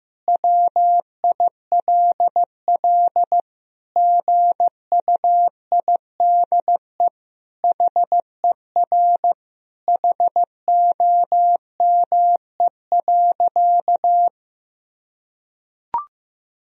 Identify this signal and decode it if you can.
Morse code, heard as a keyed tone, about 16 wpm, 700 Hz. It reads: WILL GUIDE HER HOME.
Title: Sentences from Top 1000 Words - Review Part 3 - 15wpm